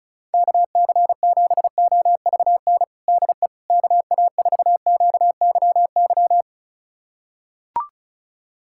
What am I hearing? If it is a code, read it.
Morse code, heard as a keyed tone, about 34 wpm, 700 Hz. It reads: KC7OVN DE KA4QYY
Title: USA Callsign DE Callsign 35wpm